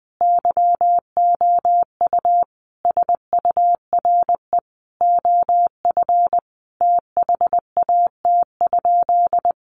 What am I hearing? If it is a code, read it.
Morse code, heard as a keyed tone, about 20 wpm, 700 Hz. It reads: YOU SURE OF THAT?
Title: Sentences from Top 300 Words 20wpm